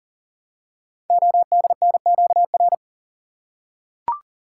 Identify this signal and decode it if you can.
Morse code, heard as a keyed tone, about 40 wpm, 690 Hz. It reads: ODNQR
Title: Code Groups - 5 Letters - 40wpm